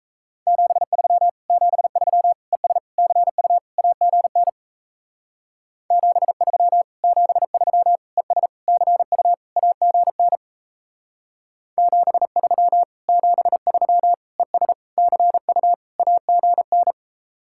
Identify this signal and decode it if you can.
Morse code, heard as a keyed tone, about 37 wpm, 700 Hz. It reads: 73 73 ES CU AGN 73 73 ES CU AGN 73 73 ES CU AGN